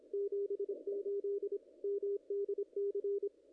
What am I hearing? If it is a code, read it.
Morse code, heard as a keyed tone, about 25 wpm, 410 Hz. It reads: 78 MDC
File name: contests / ss-cw-k5tr-op-n5rz -0100